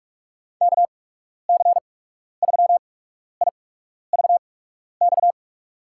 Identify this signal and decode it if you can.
Morse code, heard as a keyed tone, about 42 wpm, spaced as for 12 wpm, 700 Hz. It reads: KC3IVX
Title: USA Call Signs - 6 Characters - 8x Character Spacing - 45wpm